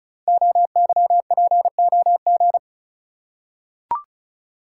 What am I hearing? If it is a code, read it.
Morse code, heard as a keyed tone, about 36 wpm, 700 Hz. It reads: OYPOG
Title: Code Groups - 5 Letters - 35wpm